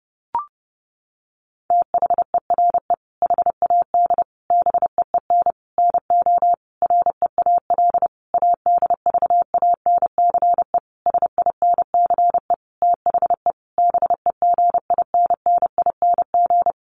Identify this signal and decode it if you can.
Morse code, heard as a keyed tone, about 28 wpm, 700 Hz. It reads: THERE HAD BEEN NO REAL ADVANCE SINCE THE BEGINNING
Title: Sentences from Top 1400 Words 30wpm